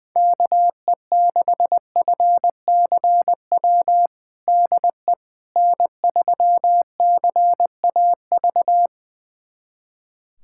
Morse code, heard as a keyed tone, about 20 wpm, 700 Hz. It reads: KE6FCW DE N3CAV
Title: USA Callsign DE Callsign 20wpm